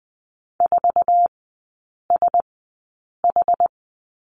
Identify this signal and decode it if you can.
Morse code, heard as a keyed tone, about 20 wpm, spaced as for 8 wpm, 690 Hz. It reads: 4SH
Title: 212 Lesson 34e - Number 8 - Three Characters 20wpm